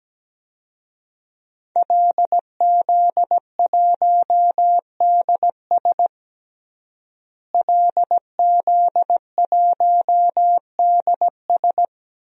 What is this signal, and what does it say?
Morse code, heard as a keyed tone, about 17 wpm, 700 Hz. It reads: LZ1DS LZ1DS
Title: DXCC-100 Call Signs - 5 Characters - Repeated 2x - 17wpm